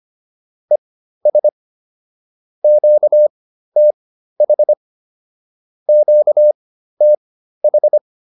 Morse code, heard as a keyed tone, about 25 wpm, spaced as for 13 wpm, 600 Hz. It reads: ES QTH QTH